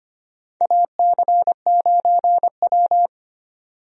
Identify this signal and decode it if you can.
Morse code, heard as a keyed tone, about 25 wpm, 700 Hz. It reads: AC9W
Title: USA Call Signs - 4 Characters - 25wpm